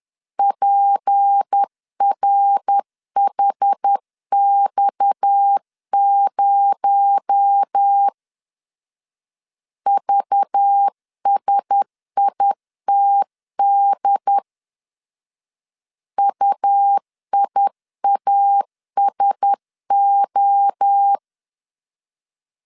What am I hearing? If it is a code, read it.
Morse code, heard as a keyed tone, about 10 wpm, 800 Hz. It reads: PRHX0 VSITD UIASO
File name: CW / DET / 1 6 1500C S C 10wpm 4252